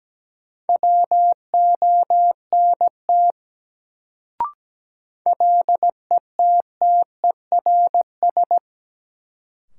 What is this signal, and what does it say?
Morse code, heard as a keyed tone, about 17 wpm, 700 Hz. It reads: WONT LETTERS
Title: Sentences from Top 1000 Words - Review Part 6 - New Word Drill - 17wpm